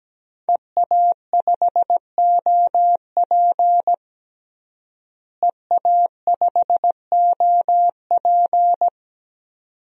Morse code, heard as a keyed tone, about 17 wpm, 700 Hz. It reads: EA5OP EA5OP
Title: DXCC-100 Call Signs - 5 Characters - Repeated 2x - 17wpm